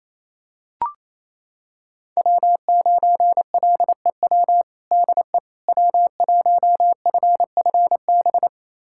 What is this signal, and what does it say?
Morse code, heard as a keyed tone, about 28 wpm, 700 Hz. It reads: W9LEW DE W1FFB
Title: USA Callsign DE Callsign - 5 Characters - 28wpm